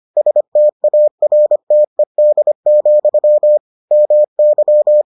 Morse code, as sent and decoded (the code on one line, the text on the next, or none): ... - .- .-. - . -.. --..-- / -- -.--
STARTED, MY